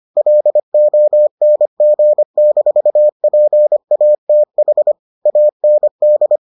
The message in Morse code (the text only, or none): LONG-PATH AND